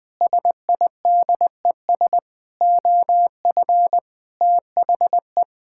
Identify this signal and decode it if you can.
Morse code, heard as a keyed tone, about 21 wpm, 700 Hz. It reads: SIDES OF THE
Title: Sentences with 4-Word Phrases - 20wpm